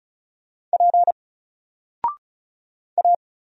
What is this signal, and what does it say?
Morse code, heard as a keyed tone, about 35 wpm, 700 Hz. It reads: P A
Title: Single Character (Letter:Number:Punctuation:Prosign) 35wpm